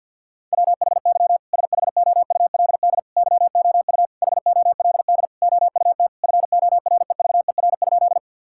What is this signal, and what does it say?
Morse code, heard as a keyed tone, about 47 wpm, 700 Hz. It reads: WHY SHOULD YOU HOLD OUT FOREVER?
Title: Sentences from Top 1600 Words 50wpm